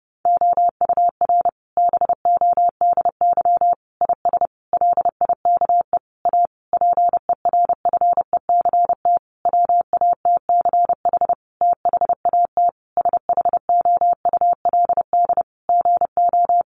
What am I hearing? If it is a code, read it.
Morse code, heard as a keyed tone, about 30 wpm, 700 Hz. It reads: OUR BODY IS LIKE A PERFECT WATCH THAT SHOULD GO
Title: Sentences from Top 1000 Words 30wpm